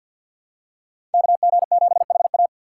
Morse code, heard as a keyed tone, about 52 wpm, 700 Hz. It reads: KG7HA
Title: USA Call Signs - 3x Character Spacing - 50wpm